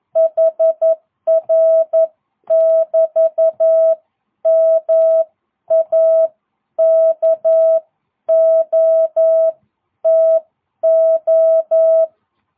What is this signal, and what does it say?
Morse code, heard as a keyed tone, about 11 wpm, spaced as for 9 wpm, 660 Hz. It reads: HR <BT> MAKOTO